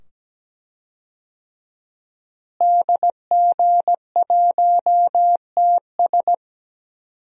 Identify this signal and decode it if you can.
Morse code, heard as a keyed tone, about 17 wpm, 700 Hz. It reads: DG1TS
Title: DXCC-100 Call Signs - 5 Characters - Repeated 2x - 17wpm